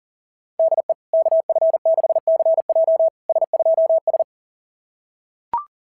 Morse code, heard as a keyed tone, about 40 wpm, 690 Hz. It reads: DE KF6CJ S2S
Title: SOTA QSO Practice 40wpm